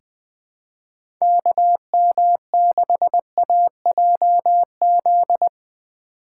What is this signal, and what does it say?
Morse code, heard as a keyed tone, about 20 wpm, 700 Hz. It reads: KM6AJZ
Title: USA Call Signs - 6 Characters - 6x Character Spacing - 20wpm